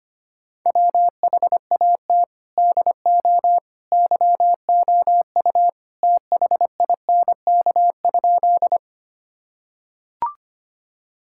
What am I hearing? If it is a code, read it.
Morse code, heard as a keyed tone, about 25 wpm, 700 Hz. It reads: WHAT DO YOU THINK?
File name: practice / 25wpm / Sentences with 2-Word Phrases - 25wpm